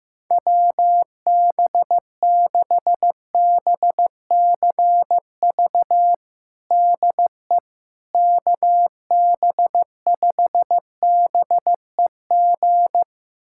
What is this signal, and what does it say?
Morse code, heard as a keyed tone, about 15 wpm, 700 Hz. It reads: WB6BCV DE KB5BEG
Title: USA Callsign DE Callsign - 6 Characters - 15wpm